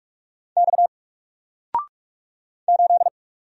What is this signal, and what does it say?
Morse code, heard as a keyed tone, about 44 wpm, 700 Hz. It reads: X 8
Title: Single Character (Letter:Number:Punctuation:Prosign) 45wpm